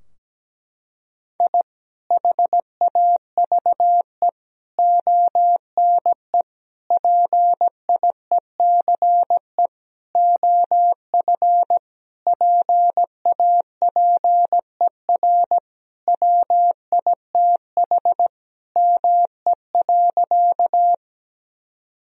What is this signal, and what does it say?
Morse code, heard as a keyed tone, about 17 wpm, 700 Hz. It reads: I HAVE ONE PIECE OF PAPER WITH ME.
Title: Sentences from Top 1000 Words - Review Part 2 - 17wpm